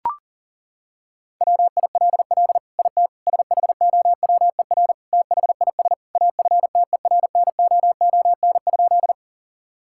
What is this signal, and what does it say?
Morse code, heard as a keyed tone, about 40 wpm, 700 Hz. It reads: WILL IT SHOWER THIS AFTERNOON?
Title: Sentences from Top 1600 Words 40wpm